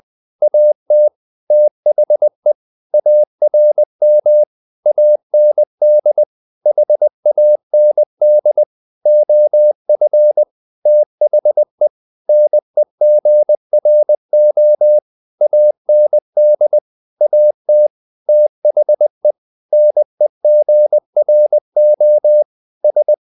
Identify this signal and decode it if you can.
Morse code, heard as a keyed tone, about 20 wpm, 600 Hz. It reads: AT THE ARM AND HAND OF THE NEGRO AND AT THE NEGRO S